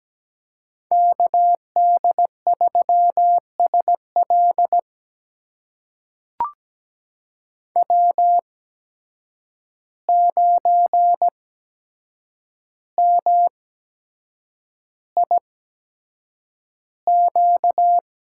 Morse code, heard as a keyed tone, about 17 wpm, spaced as for 7 wpm, 700 Hz. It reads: KD3SL W9MIQ
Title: USA Call Signs - 5 Characters - 8x Character Spacing - 17wpm